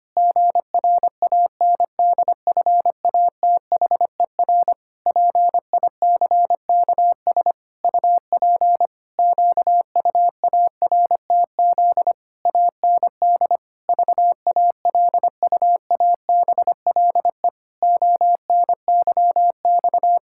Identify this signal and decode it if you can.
Morse code, heard as a keyed tone, about 25 wpm, 700 Hz. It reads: GRANDFATHER PICKS UP QUARTZ AND VALUABLE ONYX
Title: Pangram Challenge v2 - 25wpm